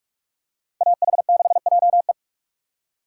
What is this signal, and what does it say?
Morse code, heard as a keyed tone, about 46 wpm, 700 Hz. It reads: AH6JE
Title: USA Call Signs - 5 Characters - 2x Character Spacing - 45wpm